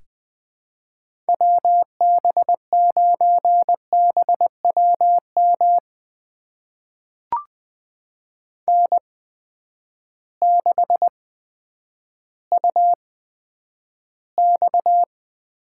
Morse code, heard as a keyed tone, about 20 wpm, spaced as for 10 wpm, 700 Hz. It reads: WB9BWM N6UX
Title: USA Call Signs - 8x Character Spacing - 20wpm